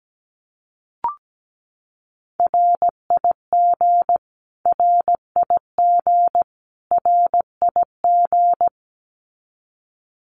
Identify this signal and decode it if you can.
Morse code, heard as a keyed tone, about 17 wpm, 700 Hz. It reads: RIG RIG RIG
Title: Abbreviations Repeated 3x - 17wpm